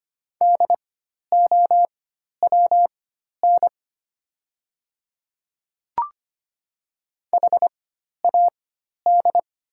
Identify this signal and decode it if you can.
Morse code, heard as a keyed tone, about 25 wpm, spaced as for 12 wpm, 700 Hz. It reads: DOWN HAD
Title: 4x Character Spacing - Top 100 Words - 25wpm